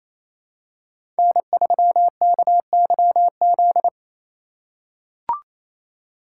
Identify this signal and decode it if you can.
Morse code, heard as a keyed tone, about 28 wpm, 700 Hz. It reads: N3KYZ
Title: USA Call Signs - 2x Character Spacing - 28wpm